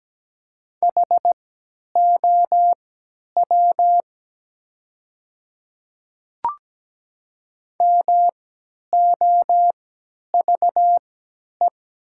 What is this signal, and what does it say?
Morse code, heard as a keyed tone, about 17 wpm, spaced as for 10 wpm, 700 Hz. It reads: HOW MOVE
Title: 3x Character Spacing - Top 300 Words - 17wpm